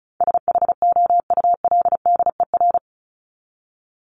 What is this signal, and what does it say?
Morse code, heard as a keyed tone, about 34 wpm, 700 Hz. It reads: SHOULDER
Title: Top 1501-1750 Words 35wpm